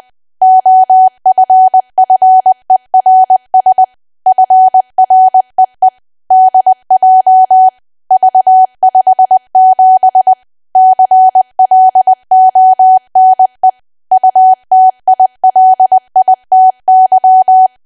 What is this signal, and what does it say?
Morse code, heard as a keyed tone, about 20 wpm, 750 Hz. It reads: OFFERS FREE DJ V57 CLONE UTILITY
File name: Archive / 20 WPM / 20WPM